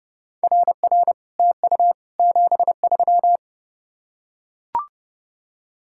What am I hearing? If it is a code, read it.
Morse code, heard as a keyed tone, about 29 wpm, 700 Hz. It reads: RR TU 73